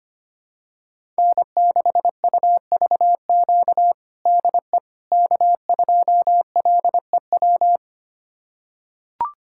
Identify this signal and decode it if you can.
Morse code, heard as a keyed tone, about 25 wpm, 700 Hz. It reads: N6UVQ DE K2LEW
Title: USA Callsign DE Callsign - 5 Characters - 25wpm